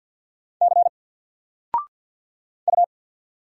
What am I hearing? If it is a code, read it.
Morse code, heard as a keyed tone, about 49 wpm, 700 Hz. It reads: C U
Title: Single Letter-Number Alt 50wpm